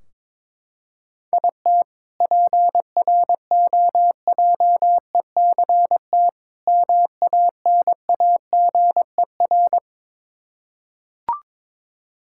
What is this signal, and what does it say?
Morse code, heard as a keyed tone, about 22 wpm, 700 Hz. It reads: IT PROJECT MANAGER
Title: K1USN SST QSO Practice 22wpm